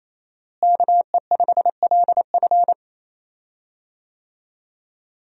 Morse code, heard as a keyed tone, about 28 wpm, 700 Hz. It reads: KE5LF
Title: USA Call Signs - 5 Characters - Repeated 2x - 28wpm